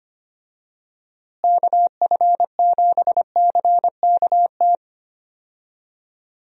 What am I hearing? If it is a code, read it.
Morse code, heard as a keyed tone, about 25 wpm, 700 Hz. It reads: KF7CKT
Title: USA Call Signs - 6 Characters - Repeated 2x - 25wpm